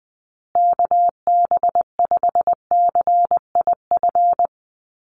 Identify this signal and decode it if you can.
Morse code, heard as a keyed tone, about 20 wpm, 700 Hz. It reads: KB5CIF
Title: USA Call Signs Repeated 3x 20wpm